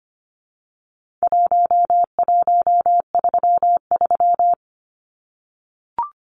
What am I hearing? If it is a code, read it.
Morse code, heard as a keyed tone, about 25 wpm, 700 Hz. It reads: 1133
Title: Numbers - Contest Style - 4-Digit 25wpm